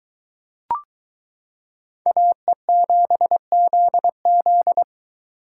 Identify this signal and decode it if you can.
Morse code, heard as a keyed tone, about 23 wpm, 700 Hz. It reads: AE7ZZ